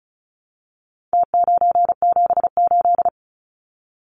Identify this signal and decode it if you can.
Morse code, heard as a keyed tone, about 34 wpm, 700 Hz. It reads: T978
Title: Numbers - Contest Style - 4-Digit 35wpm